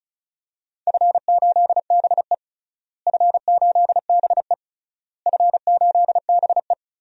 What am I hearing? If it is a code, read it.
Morse code, heard as a keyed tone, about 36 wpm, 700 Hz. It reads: F8BE F8BE F8BE
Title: DXCC-100 Call Signs - 4 Characters - Repeated 3x - 35wpm